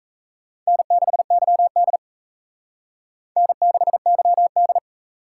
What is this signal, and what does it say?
Morse code, heard as a keyed tone, about 35 wpm, 700 Hz. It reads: N6YD N6YD
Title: Speed Racing - USA 1x2 Call Signs 28wpm